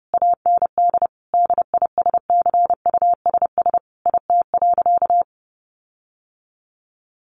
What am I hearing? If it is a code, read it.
Morse code, heard as a keyed tone, about 30 wpm, 700 Hz. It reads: AND DISCUSS IT.
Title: Sentences from Top 1400 Words 30wpm